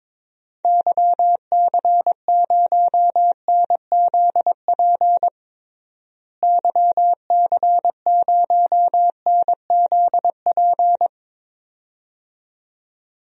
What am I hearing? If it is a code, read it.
Morse code, heard as a keyed tone, about 22 wpm, 700 Hz. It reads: YC0NZP YC0NZP
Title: DXCC-100 Call Signs - 6 Characters - Repeated 2x - 22wpm